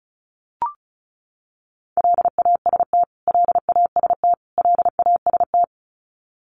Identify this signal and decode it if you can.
Morse code, heard as a keyed tone, about 34 wpm, 700 Hz. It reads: LAST LAST LAST
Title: Top 300 Words Repeated 35wpm